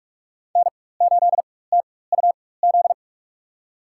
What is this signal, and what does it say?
Morse code, heard as a keyed tone, about 46 wpm, spaced as for 21 wpm, 700 Hz. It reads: N8TUZ
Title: USA Call Signs - 5 Characters - 4x Character Spacing - 45wpm